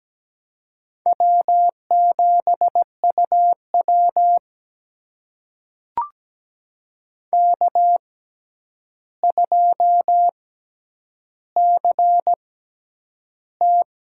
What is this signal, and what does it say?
Morse code, heard as a keyed tone, about 17 wpm, spaced as for 9 wpm, 700 Hz. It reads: W7UW K2CT
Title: USA Call Signs - 4 Characters - 6x Character Spacing - 17wpm